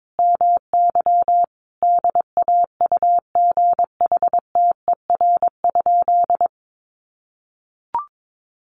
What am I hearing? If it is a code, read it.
Morse code, heard as a keyed tone, about 22 wpm, 700 Hz. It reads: MY DAUGHTER?
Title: Sentences from Top 1500 Words 22wpm